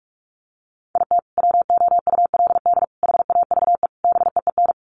Morse code, heard as a keyed tone, about 44 wpm, 700 Hz. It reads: IT WOULD HAVE BEEN
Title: Sets of 4 Words 45wpm